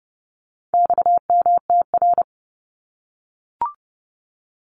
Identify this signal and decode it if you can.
Morse code, heard as a keyed tone, about 30 wpm, 700 Hz. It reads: XMTR
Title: Abbreviations Repeated 3x - 30wpm